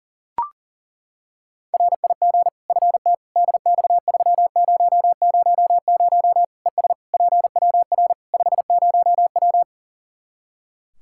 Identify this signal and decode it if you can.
Morse code, heard as a keyed tone, about 37 wpm, 700 Hz. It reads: RIG FT DX3000 ES PWR 50W
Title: Practice QSO - Exchange 2 - Info - 40wpm